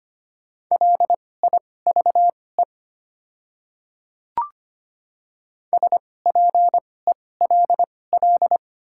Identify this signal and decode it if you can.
Morse code, heard as a keyed tone, about 25 wpm, spaced as for 18 wpm, 690 Hz. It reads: LIVE SPELL
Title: 2x Character Spacing - Top 200 Words - 25wpm